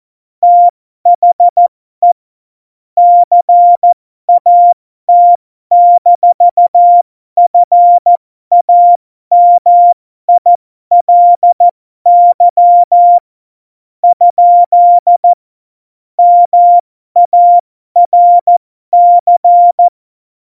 Morse code, heard as a keyed tone, about 14 wpm, spaced as for 12 wpm, 700 Hz. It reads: THE CAT-FAMILY ? MARC
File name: Morsekurs / Witze